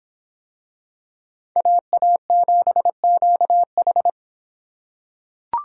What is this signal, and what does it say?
Morse code, heard as a keyed tone, about 25 wpm, 700 Hz. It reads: AA7QH